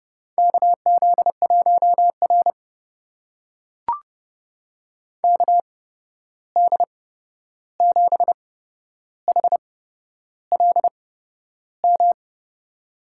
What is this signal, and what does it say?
Morse code, heard as a keyed tone, about 32 wpm, spaced as for 11 wpm, 700 Hz. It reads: KZ1R KD7HLM
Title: USA Call Signs - 8x Character Spacing - 30wpm